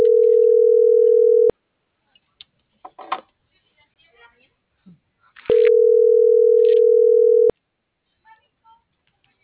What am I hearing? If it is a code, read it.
Ringback tone (North America), 440 + 480 Hz.